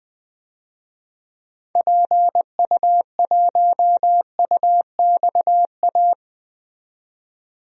Morse code, heard as a keyed tone, about 20 wpm, 700 Hz. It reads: PU1UXA
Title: DXCC-100 Call Signs - 6 Characters - 20wpm